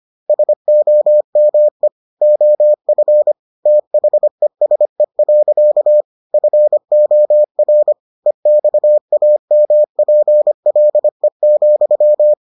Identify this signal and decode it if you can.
Morse code, heard as a keyed tone, about 25 wpm, 600 Hz. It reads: SOME OF THESE. FOR EXAMPLE,